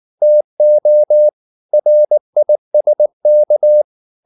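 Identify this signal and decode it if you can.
Morse code, heard as a keyed tone, about 19 wpm, 600 Hz. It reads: TO RISK
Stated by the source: Morse code at 19 wpm from CW med Gnister, reading Ken Follett's novel Never in English.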